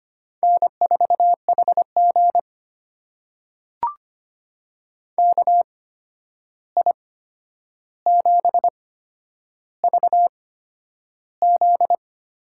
Morse code, heard as a keyed tone, about 25 wpm, spaced as for 10 wpm, 700 Hz. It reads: N4HG KI7VZ
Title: USA Call Signs - 8x Character Spacing - 25wpm